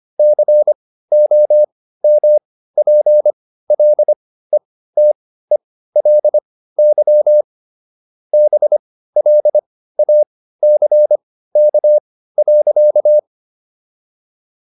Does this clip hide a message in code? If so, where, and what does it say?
Yes: Morse code from 0.2 to 13.2 s, heard as a keyed tone, about 25 wpm, spaced as for 15 wpm, 600 Hz. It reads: COMPLETELY BLACK.